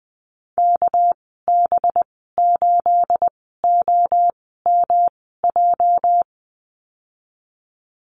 Morse code, heard as a keyed tone, about 20 wpm, spaced as for 14 wpm, 700 Hz. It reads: KB8OMJ
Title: 245 Lesson 39g - Letter z - Call Signs 20wpm